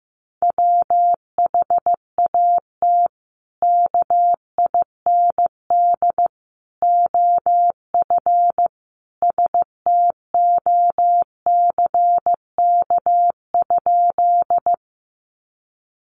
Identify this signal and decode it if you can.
Morse code, heard as a keyed tone, about 15 wpm, 700 Hz. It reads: WHAT KIND OF STOCK?
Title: Sentences from Top 800 Words 15wpm